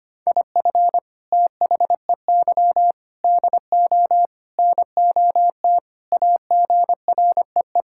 Morse code, heard as a keyed tone, about 25 wpm, 700 Hz. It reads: IF THEY DO NOT AGREE
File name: practice / 25wpm / Sentences with 4-Word Phrases - 25wpm